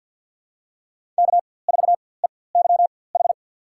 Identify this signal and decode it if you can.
Morse code, heard as a keyed tone, about 47 wpm, spaced as for 23 wpm, 700 Hz. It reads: K4EYH
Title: USA Call Signs - 4x Character Spacing - 50wpm